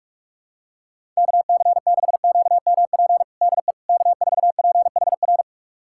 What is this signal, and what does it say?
Morse code, heard as a keyed tone, about 46 wpm, 700 Hz. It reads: KC6QMP DE K4PHR
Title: USA Callsign DE Callsign 45wpm